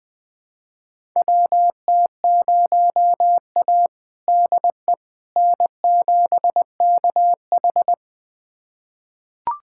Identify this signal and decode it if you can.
Morse code, heard as a keyed tone, about 20 wpm, 700 Hz. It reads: WT0A DE N7KH
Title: USA Callsign DE Callsign - 4 Characters - 20wpm